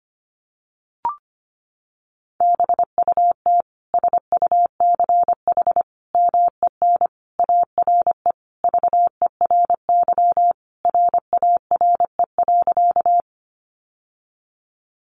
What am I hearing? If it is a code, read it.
Morse code, heard as a keyed tone, about 25 wpm, 700 Hz. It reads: BUT SUCH MEN ARE VERY RARE.
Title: Sentences from Top 1100 Words 25wpm